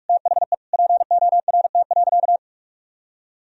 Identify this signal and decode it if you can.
Morse code, heard as a keyed tone, about 46 wpm, 700 Hz. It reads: THE PORT.
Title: Sentences from Top 1000 Words - Review Part 2 - 45wpm